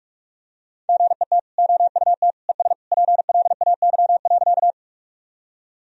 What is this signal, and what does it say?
Morse code, heard as a keyed tone, about 46 wpm, 700 Hz. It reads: GET OUT ES PLAY.
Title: Sentences from Top 200 Words and Abbreviations 45wpm